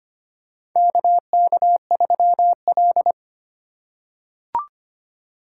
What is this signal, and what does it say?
Morse code, heard as a keyed tone, about 25 wpm, 700 Hz. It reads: KK3L
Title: USA Call Signs - 4 Characters - 8x Character Spacing - 25wpm